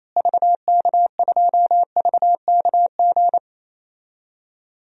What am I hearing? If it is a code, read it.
Morse code, heard as a keyed tone, about 28 wpm, 700 Hz. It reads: VK2VKG
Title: DXCC-100 Call Signs - 6 Characters - Repeated 2x - 28wpm